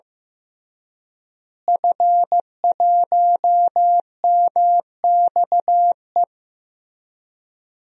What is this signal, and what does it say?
Morse code, heard as a keyed tone, about 15 wpm, 700 Hz. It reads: F1MXE
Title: DXCC-100 Call Signs - 5 Characters - Repeated 3x - 15wpm